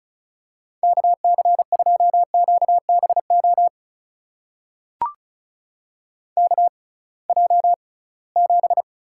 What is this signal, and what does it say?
Morse code, heard as a keyed tone, about 36 wpm, spaced as for 23 wpm, 700 Hz. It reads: KC2QBO KJ7
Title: USA Call Signs - 6 Characters - 6x Character Spacing - 35wpm